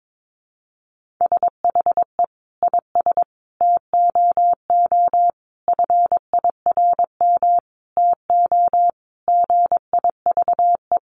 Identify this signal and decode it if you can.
Morse code, heard as a keyed tone, about 22 wpm, 700 Hz. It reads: SHE IS TOO FIRM TO GIVE
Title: Sentences from Top 1400 Words 22wpm